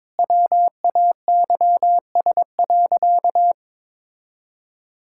Morse code, heard as a keyed tone, about 23 wpm, 700 Hz. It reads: WAYS.
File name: practice / 22wpm / Sentences with 2-Word Phrases - 22wpm